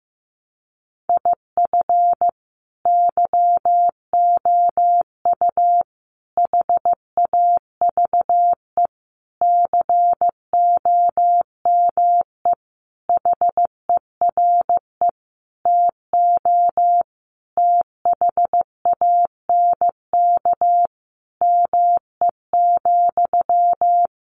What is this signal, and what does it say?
Morse code, heard as a keyed tone, about 15 wpm, 700 Hz. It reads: IF YOU HAVE COME HERE TO THANK ME,
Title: Sentences from Top 800 Words 15wpm